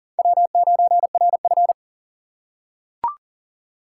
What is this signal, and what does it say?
Morse code, heard as a keyed tone, about 40 wpm, 700 Hz. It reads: W9RF
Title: USA Call Signs - 4 Characters - 3x Character Spacing - 40wpm